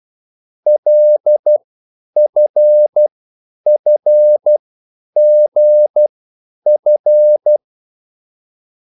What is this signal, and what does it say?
Morse code, heard as a keyed tone, about 12 wpm, spaced as for 9 wpm, 600 Hz. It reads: LFFGF
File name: QSL / CW lesson